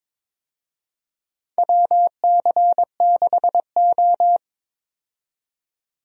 Morse code, heard as a keyed tone, about 22 wpm, 700 Hz. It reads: WC6O
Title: USA Call Signs - 4 Characters - 22wpm